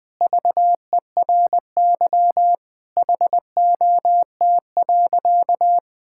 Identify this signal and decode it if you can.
Morse code, heard as a keyed tone, about 20 wpm, 700 Hz. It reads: VERY HOT.
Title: Sentences with 5-Word Phrases - 20wpm